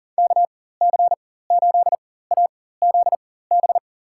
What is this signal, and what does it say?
Morse code, heard as a keyed tone, about 40 wpm, spaced as for 19 wpm, 700 Hz. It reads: KC8AZB
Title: USA Call Signs - 6 Characters - 4x Character Spacing - 40wpm